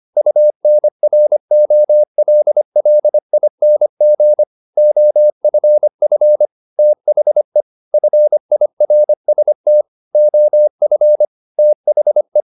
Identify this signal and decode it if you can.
Morse code, heard as a keyed tone, about 25 wpm, 600 Hz. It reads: UNROLLING OFF THE FIRST OF THE
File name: the old man and the sea part 1 25wpm